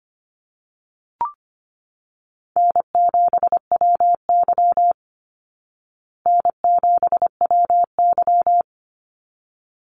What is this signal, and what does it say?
Morse code, heard as a keyed tone, about 25 wpm, 700 Hz. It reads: N7WY N7WY
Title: Call Signs Repeated 3x - 25wpm